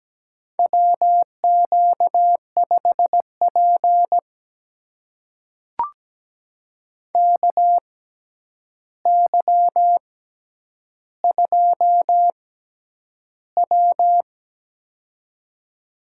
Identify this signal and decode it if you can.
Morse code, heard as a keyed tone, about 17 wpm, spaced as for 9 wpm, 700 Hz. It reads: WQ5P KY2W
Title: USA Call Signs - 6x Character Spacing - 17wpm